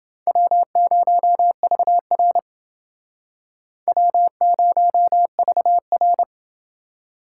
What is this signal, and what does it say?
Morse code, heard as a keyed tone, about 29 wpm, 700 Hz. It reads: W0VR W0VR